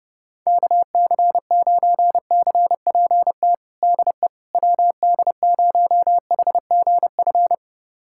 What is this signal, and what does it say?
Morse code, heard as a keyed tone, about 30 wpm, 700 Hz. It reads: KC9CPT DE WD0HGF
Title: USA Callsign DE Callsign - 6 Characters - 30wpm